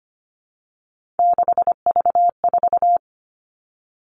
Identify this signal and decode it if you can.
Morse code, heard as a keyed tone, about 25 wpm, 700 Hz. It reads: 6V4
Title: Three Challenging Letter-Number Combinations 25wpm